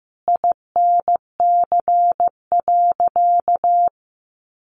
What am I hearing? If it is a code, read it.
Morse code, heard as a keyed tone, about 15 wpm, 700 Hz. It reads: INC.
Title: A Sound Of Thunder 15wpm